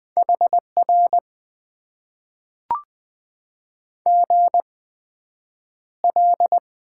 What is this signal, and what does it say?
Morse code, heard as a keyed tone, about 20 wpm, spaced as for 9 wpm, 700 Hz. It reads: HR GL
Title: Top 90 QSO Elements - 8x Character Spacing - 20wpm